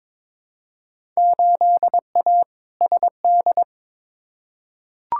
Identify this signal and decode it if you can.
Morse code, heard as a keyed tone, about 21 wpm, 700 Hz. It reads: 8A SD
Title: Field Day Exchanges Encore - 22wpm